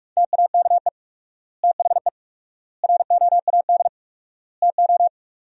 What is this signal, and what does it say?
Morse code, heard as a keyed tone, about 46 wpm, 700 Hz. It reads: TAKE THE ROAD TO
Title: 4x Word-Spacing - Sentences from Top 400 Words - 45wpm